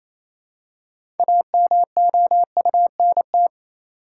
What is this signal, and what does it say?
Morse code, heard as a keyed tone, about 28 wpm, 700 Hz. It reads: AMOUNT
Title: Six Letter Words - 28wpm